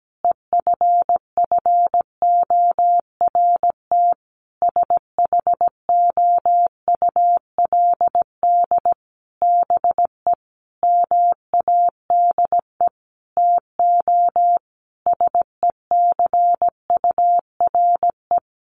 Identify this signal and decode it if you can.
Morse code, heard as a keyed tone, about 17 wpm, 700 Hz. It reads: EFFORT SHOULD BE MADE TO SECURE
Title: Sentences from Top 1200 Words 17wpm